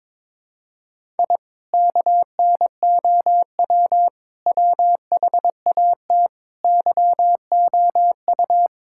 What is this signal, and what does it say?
Morse code, heard as a keyed tone, about 22 wpm, 700 Hz. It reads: I KNOW WHAT YOU
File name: practice / 22wpm / Sentences with 2-Word Phrases - 22wpm